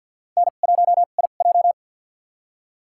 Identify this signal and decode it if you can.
Morse code, heard as a keyed tone, about 52 wpm, spaced as for 36 wpm, 700 Hz. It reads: N1IJ
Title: USA Call Signs - 4 Characters - 2x Character Spacing - 50wpm